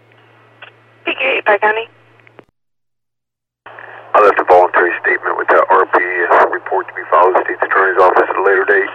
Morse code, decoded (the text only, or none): none